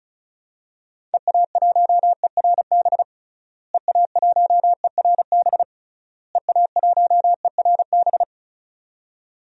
Morse code, heard as a keyed tone, about 34 wpm, 700 Hz. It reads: EA1ERB EA1ERB EA1ERB
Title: DXCC-100 Call Signs - 6 Characters - Repeated 3x - 35wpm